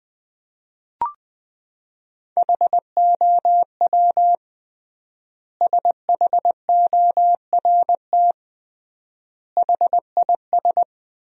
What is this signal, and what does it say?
Morse code, heard as a keyed tone, about 20 wpm, 700 Hz. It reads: HOW SHORT HIS
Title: 3x Word-Spacing - Sentences from Top 400 Words - 20wpm